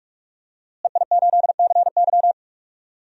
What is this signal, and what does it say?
Morse code, heard as a keyed tone, about 41 wpm, 700 Hz. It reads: EI8CY
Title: DXCC-100 Call Signs - 5 Characters - Repeated 3x - 45wpm